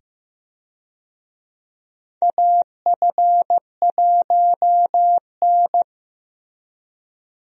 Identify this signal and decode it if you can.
Morse code, heard as a keyed tone, about 15 wpm, 700 Hz. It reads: AF1N
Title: USA Call Signs - 4 Characters - 15wpm